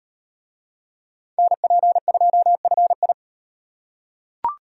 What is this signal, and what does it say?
Morse code, heard as a keyed tone, about 36 wpm, 700 Hz. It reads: NP2FI